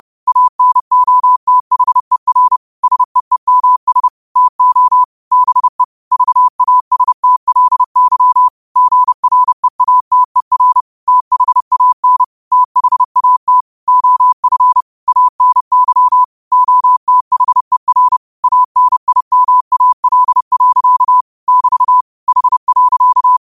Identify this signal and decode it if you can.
Morse code, heard as a keyed tone, about 30 wpm, 1000 Hz. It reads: ANOTHER SEEMS TO BE VASTLY GREATER THAN THAT OF ANY OTHER ANIMAL. <BT> H.
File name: qotdfiles / 30wpm